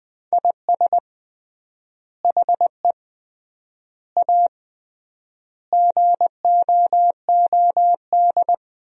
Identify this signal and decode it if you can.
Morse code, heard as a keyed tone, about 20 wpm, 700 Hz. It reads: IS HE A GOOD
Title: 3x Word-Spacing - Sentences from Top 200 Words - 20wpm